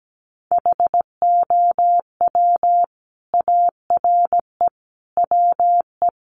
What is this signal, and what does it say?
Morse code, heard as a keyed tone, about 17 wpm, 700 Hz. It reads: HOW ARE WE
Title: Sets of 5 Words Encore 17wpm